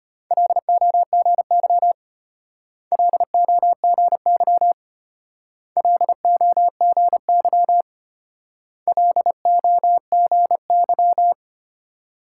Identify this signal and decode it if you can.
Morse code, heard as a keyed tone, about 31 wpm, 700 Hz. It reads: LOGY LOGY LOGY LOGY
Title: Speed-Racing Multi-letter Phonemes - 25wpm